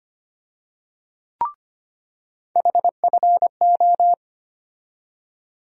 Morse code, heard as a keyed tone, about 25 wpm, 700 Hz. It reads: HFO
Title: USA Call Sign Postfixes - 3 Letters 25wpm